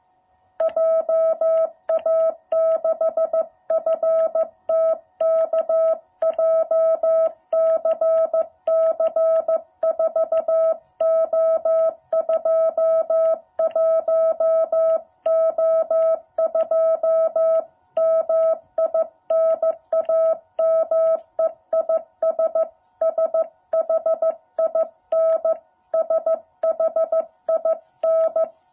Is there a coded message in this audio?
Yes: Morse code from 0.6 to 28.4 s, heard as a keyed tone, about 14 wpm, 660 Hz. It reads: JA6FTK JCC 4O21O2 MI NAME IS SHIN SHIN